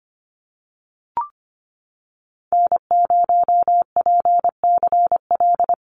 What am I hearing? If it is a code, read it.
Morse code, heard as a keyed tone, about 25 wpm, 700 Hz. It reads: N0PCL
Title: NA SOTA Activator Callsigns 25wpm